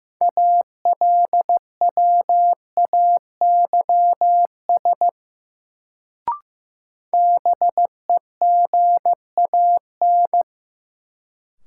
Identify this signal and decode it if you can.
Morse code, heard as a keyed tone, about 15 wpm, 700 Hz. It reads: ALWAYS BEGAN
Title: Sentences from Top 300 Words - New Word Drill - 15wpm